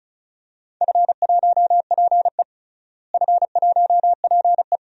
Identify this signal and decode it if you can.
Morse code, heard as a keyed tone, about 34 wpm, 700 Hz. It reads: F1PE F1PE
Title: DXCC-100 Call Signs - 4 Characters - Repeated 3x - 35wpm